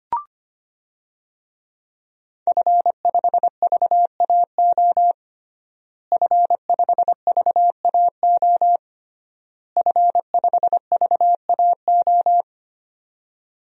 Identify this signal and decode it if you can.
Morse code, heard as a keyed tone, about 25 wpm, 700 Hz. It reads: F5VAO F5VAO F5VAO
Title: DXCC-100 Call Signs - 5 Characters - Repeated 3x - 25wpm